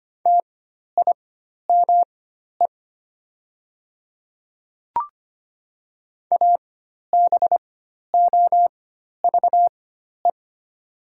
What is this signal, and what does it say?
Morse code, heard as a keyed tone, about 25 wpm, spaced as for 12 wpm, 690 Hz. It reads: TIME ABOVE
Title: 4x Character Spacing - Top 500 Words - 25wpm